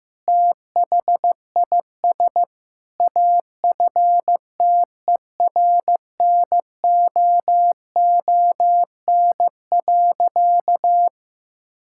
Morse code, heard as a keyed tone, about 15 wpm, 700 Hz. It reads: THIS AFTERNOON.
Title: Sentences from Top 1600 Words 15wpm